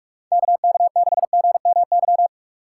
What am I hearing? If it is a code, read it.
Morse code, heard as a keyed tone, about 43 wpm, 700 Hz. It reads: KK6GMY